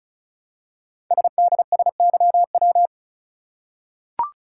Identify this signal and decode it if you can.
Morse code, heard as a keyed tone, about 36 wpm, 700 Hz. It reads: SDSYW
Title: Code Groups - 5 Letter:Numbers - 35wpm